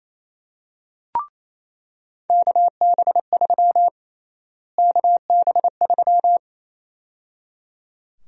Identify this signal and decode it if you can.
Morse code, heard as a keyed tone, about 28 wpm, 700 Hz. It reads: KB3 KB3
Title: USA Call Sign Prefixes - Repeated 2x - 28wpm